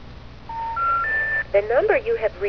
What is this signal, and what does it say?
Special information tone, with speech.